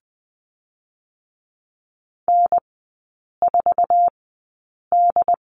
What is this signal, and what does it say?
Morse code, heard as a keyed tone, about 20 wpm, spaced as for 8 wpm, 700 Hz. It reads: N4D
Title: 067 Lesson 12e - Letter d - Three Characters 20wpm